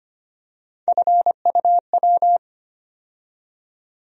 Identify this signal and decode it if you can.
Morse code, heard as a keyed tone, about 25 wpm, 700 Hz. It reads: FUW
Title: USA Call Sign Postfixes - 3 Letters 25wpm